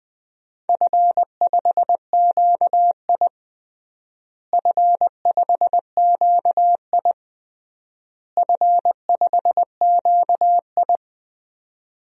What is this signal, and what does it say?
Morse code, heard as a keyed tone, about 20 wpm, 700 Hz. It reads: F5QI F5QI F5QI
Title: DXCC-100 Call Signs - 4 Characters - Repeated 3x - 20wpm